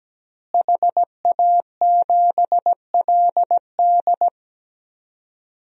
Morse code, heard as a keyed tone, about 17 wpm, 700 Hz. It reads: HA7LD
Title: DXCC-100 Call Signs - 5 Characters - Repeated 3x - 17wpm